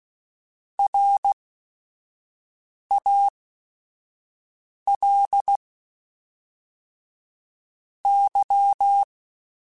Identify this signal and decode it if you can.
Morse code, heard as a keyed tone, about 16 wpm, spaced as for 5 wpm, 800 Hz. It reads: RAL Y
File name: QSOs / QSOs-mp3s / kc7sac@5